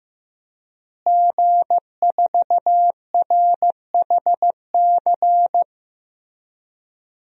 Morse code, heard as a keyed tone, about 16 wpm, 700 Hz. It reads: G4RHC
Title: DXCC-100 Call Signs - 5 Characters - Repeated 3x - 15wpm